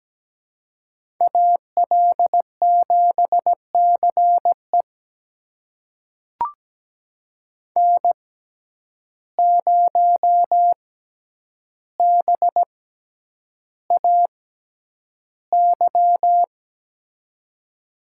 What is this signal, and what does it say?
Morse code, heard as a keyed tone, about 17 wpm, spaced as for 9 wpm, 700 Hz. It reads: AL7CE N0BAY
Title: USA Call Signs - 5 Characters - 6x Character Spacing - 17wpm